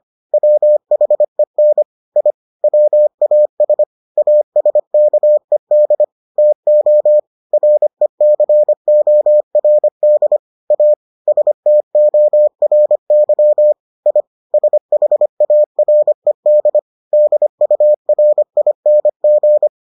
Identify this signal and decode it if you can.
Morse code, heard as a keyed tone, about 25 wpm, 600 Hz. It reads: WHEN I WAS ASKED TO RECORD A STORY I SHARED DURING